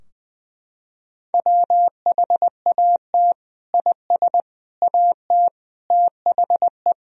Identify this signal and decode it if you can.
Morse code, heard as a keyed tone, about 20 wpm, 700 Hz. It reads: WHAT IS AT THE
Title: Sentences from Top 1000 Words - Review Part 2 - 20wpm